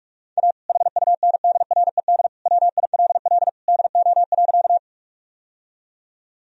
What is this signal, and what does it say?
Morse code, heard as a keyed tone, about 46 wpm, 700 Hz. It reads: A HUNDRED WILL DO.
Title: Sentences from Top 1000 Words - Review Part 2 - 45wpm